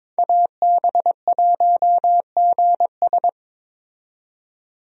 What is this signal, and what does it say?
Morse code, heard as a keyed tone, about 22 wpm, 700 Hz. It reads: AB1GS
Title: USA Call Signs - 5 Characters - 22wpm